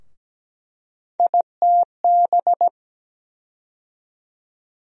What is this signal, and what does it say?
Morse code, heard as a keyed tone, about 17 wpm, 700 Hz. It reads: ITB
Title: USA Call Sign Postfixes - 3 Letters 17wpm